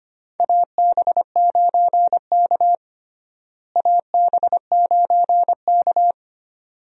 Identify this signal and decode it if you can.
Morse code, heard as a keyed tone, about 25 wpm, 700 Hz. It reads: AB9K AB9K
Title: USA Call Signs - 4 Characters - Repeated 2x - 25wpm